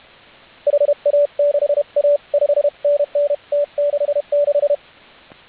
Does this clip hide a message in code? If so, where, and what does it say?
Yes: Morse code from 0.7 to 4.8 s, heard as a keyed tone, about 30 wpm, 590 Hz, with radio noise. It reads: HA6A 5NN T66